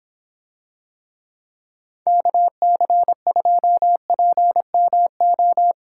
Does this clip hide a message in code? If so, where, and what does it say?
Yes: Morse code from 2.1 to 5.7 s, heard as a keyed tone, about 26 wpm, 700 Hz. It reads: KC2PMO